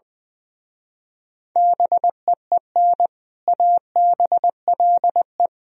Morse code, heard as a keyed tone, about 22 wpm, 700 Hz. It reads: BEEN ABLE
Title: Speed-Racing - Sets of 3 Words 20wpm